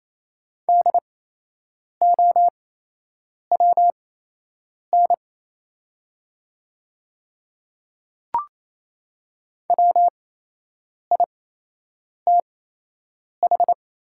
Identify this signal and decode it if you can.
Morse code, heard as a keyed tone, about 28 wpm, spaced as for 8 wpm, 700 Hz. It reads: DOWN WITH
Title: 8x Character Spacing - Top 300 Words - 28wpm